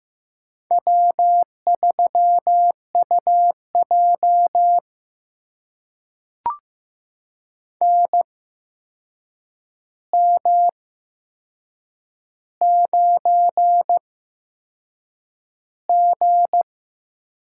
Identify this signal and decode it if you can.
Morse code, heard as a keyed tone, about 15 wpm, spaced as for 6 wpm, 700 Hz. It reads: W3UJ NM9G
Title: USA Call Signs - 4 Characters - 8x Character Spacing - 15wpm